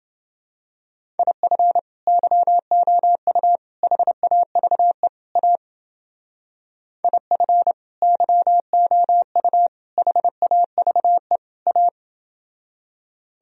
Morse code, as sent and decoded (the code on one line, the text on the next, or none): .. ..-. / -.-- --- ..- / .... .- ...- . / .- / .. ..-. / -.-- --- ..- / .... .- ...- . / .-
IF YOU HAVE A IF YOU HAVE A